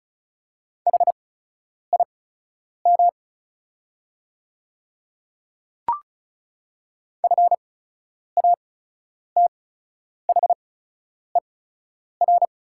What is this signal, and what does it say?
Morse code, heard as a keyed tone, about 36 wpm, spaced as for 10 wpm, 700 Hz. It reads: HIM FATHER
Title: 8x Character Spacing - Top 200 Words - 35wpm